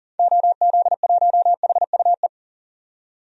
Morse code, heard as a keyed tone, about 40 wpm, 700 Hz. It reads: OZ1HUE
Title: DXCC-100 Call Signs - 6 Characters - 40wpm